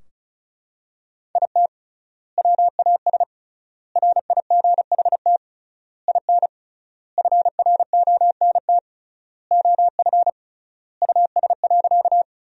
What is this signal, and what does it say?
Morse code, heard as a keyed tone, about 36 wpm, 700 Hz. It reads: IT WAS RIGHT IN FRONT OF US.
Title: 3x Word-Spacing - Sentences from Top 500 Words - 35wpm